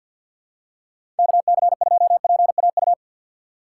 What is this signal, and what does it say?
Morse code, heard as a keyed tone, about 53 wpm, 700 Hz. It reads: KC2PAU
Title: USA Call Signs - 6 Characters - 6x Character Spacing - 50wpm